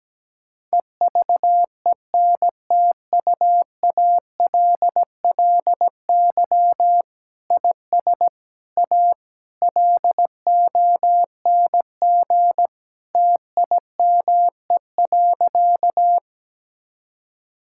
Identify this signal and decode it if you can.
Morse code, heard as a keyed tone, about 17 wpm, 700 Hz. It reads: EVENTUALLY IS A LONG TIME.
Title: Sentences from Top 1000 Words - Review Part 6 - 17wpm